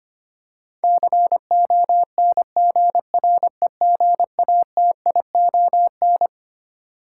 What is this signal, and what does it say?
Morse code, heard as a keyed tone, about 25 wpm, 700 Hz. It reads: CONGREGATION
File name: Twelve Letter Words - 25wpm